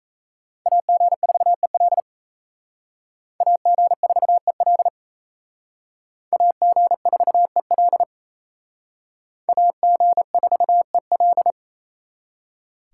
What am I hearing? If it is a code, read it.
Morse code, heard as a keyed tone, about 34 wpm, 700 Hz. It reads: AG4EL AG4EL AG4EL AG4EL
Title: Speed Racing - USA 2x2 Call Signs 28wpm